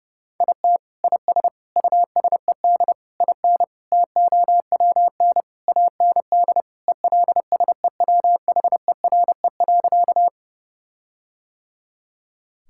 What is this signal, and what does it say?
Morse code, heard as a keyed tone, about 30 wpm, 700 Hz. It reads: IT IS USED IN TOWN AND ELSEWHERE.
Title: Sentences from Top 1600 Words 30wpm